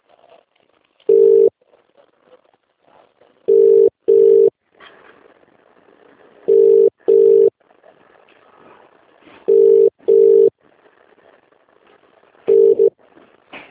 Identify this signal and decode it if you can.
Ringback tone (UK), 400 + 450 Hz.